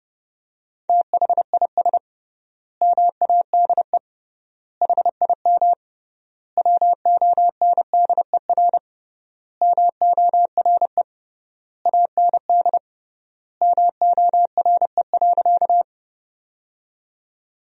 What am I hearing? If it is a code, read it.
Morse code, heard as a keyed tone, about 30 wpm, 700 Hz. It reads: THIS MADE HIM WONDER MORE AND MORE.
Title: 3x Word-Spacing - Sentences from Top 500 Words - 30wpm